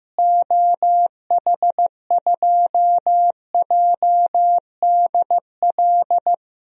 Morse code, heard as a keyed tone, about 15 wpm, 700 Hz. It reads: OH2JDL
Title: DXCC-100 Call Signs - 6 Characters - Repeated 2x - 15wpm